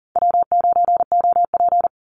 Morse code, heard as a keyed tone, about 40 wpm, 700 Hz. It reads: W9OP
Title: Call Signs - 40wpm